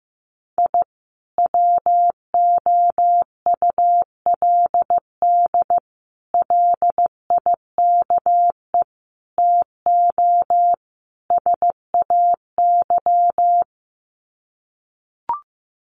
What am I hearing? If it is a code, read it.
Morse code, heard as a keyed tone, about 15 wpm, 700 Hz. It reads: I WOULD LIKE TO SAY
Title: Sets of 5 Words Encore2 15wpm